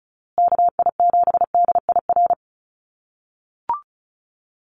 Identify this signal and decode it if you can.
Morse code, heard as a keyed tone, about 34 wpm, 700 Hz. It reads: KI7DIR
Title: USA Call Signs Repeated 3x 35wpm